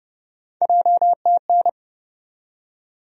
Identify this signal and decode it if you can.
Morse code, heard as a keyed tone, about 32 wpm, 700 Hz. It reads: JTN
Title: USA Call Sign Postfixes - 3 Letters 30wpm